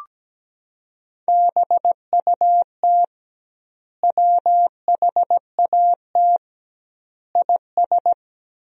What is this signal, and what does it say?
Morse code, heard as a keyed tone, about 17 wpm, 700 Hz. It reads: BUT WHAT IS
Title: 2x Word-Spacing - Sentences from Top 300 Words - 17wpm